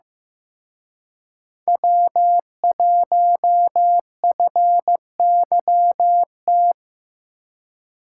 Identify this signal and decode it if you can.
Morse code, heard as a keyed tone, about 18 wpm, 700 Hz. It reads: W1FYT
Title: Speed Racing - USA 1x3 Call Signs 15wpm